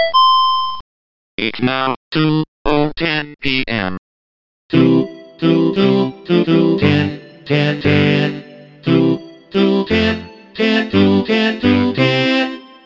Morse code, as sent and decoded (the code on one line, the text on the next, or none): none